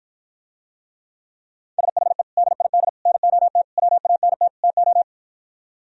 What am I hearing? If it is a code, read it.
Morse code, heard as a keyed tone, about 53 wpm, 700 Hz. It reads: SHE DID NOT WANT TO